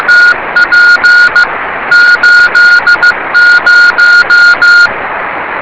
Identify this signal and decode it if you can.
Morse code, heard as a keyed tone, about 15 wpm, 1430 Hz, with noise. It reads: TP 80
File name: rs30 cw